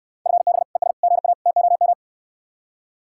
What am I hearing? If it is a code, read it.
Morse code, heard as a keyed tone, about 69 wpm, 700 Hz. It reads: FB ES GA TOM